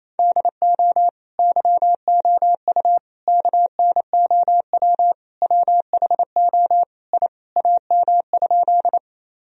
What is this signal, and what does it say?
Morse code, heard as a keyed tone, about 28 wpm, 700 Hz. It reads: DO YOU KNOW WHO I AM?
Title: Sentences from Top 1000 Words - Review Part 2 - 28wpm